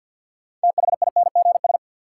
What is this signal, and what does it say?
Morse code, heard as a keyed tone, about 47 wpm, 700 Hz. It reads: THINGS
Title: Sets of 3 Words - New Word Drill - 50wpm